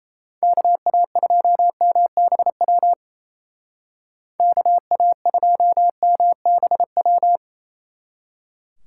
Morse code, heard as a keyed tone, about 36 wpm, 700 Hz. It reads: KA2MBW KA2MBW